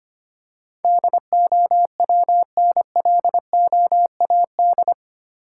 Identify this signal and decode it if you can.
Morse code, heard as a keyed tone, about 25 wpm, 700 Hz. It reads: DOWNLOAD
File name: Eight Letter Words - 25wpm